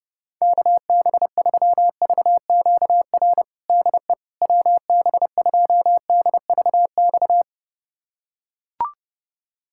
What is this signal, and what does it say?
Morse code, heard as a keyed tone, about 30 wpm, 700 Hz. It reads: KB3VQR DE WB2DVX
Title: USA Callsign DE Callsign - 6 Characters - 30wpm